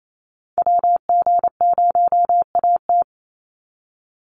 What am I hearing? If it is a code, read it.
Morse code, heard as a keyed tone, about 28 wpm, 700 Hz. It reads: WG0AT
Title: NA SOTA Chaser Callsigns 28wpm